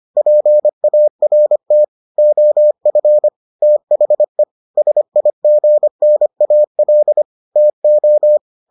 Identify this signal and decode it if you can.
Morse code, heard as a keyed tone, about 25 wpm, 600 Hz. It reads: PART OF THE SIGNAL TO